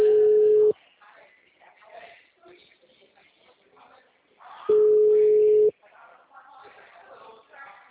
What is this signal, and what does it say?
Ringback tone (Europe, 425 Hz), with speech.
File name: out- -2005- - - .80944